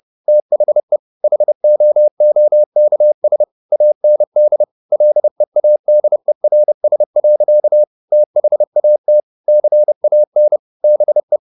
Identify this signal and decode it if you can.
Morse code, heard as a keyed tone, about 30 wpm, 600 Hz. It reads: THE HOOKS AND LEADERS. THAT CAN BE
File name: the old man and the sea / the old man and the sea 1 30wpm